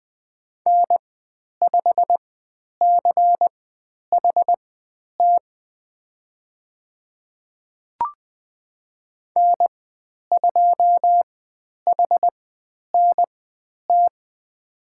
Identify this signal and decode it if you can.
Morse code, heard as a keyed tone, about 20 wpm, spaced as for 10 wpm, 700 Hz. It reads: N5CHT N2HNT
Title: 102 Lesson 17g - Letter u - Call Signs 20wpm